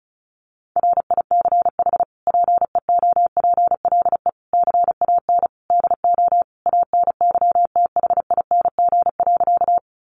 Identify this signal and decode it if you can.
Morse code, heard as a keyed tone, about 34 wpm, 700 Hz. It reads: RICH PEOPLE CAN DO ANYTHING.
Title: Sentences from Top 1100 Words 35wpm